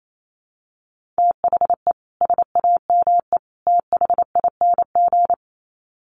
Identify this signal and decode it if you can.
Morse code, heard as a keyed tone, about 28 wpm, 700 Hz. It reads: THE SAME THING
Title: Sets of 3 Words 28wpm